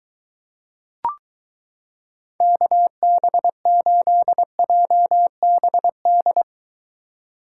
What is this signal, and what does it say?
Morse code, heard as a keyed tone, about 23 wpm, 700 Hz. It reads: KB8JBD